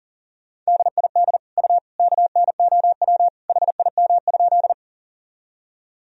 Morse code, heard as a keyed tone, about 40 wpm, 700 Hz. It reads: DID U KNOW HIM?
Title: Sentences from Top 100 Words and Abbreviations 40wpm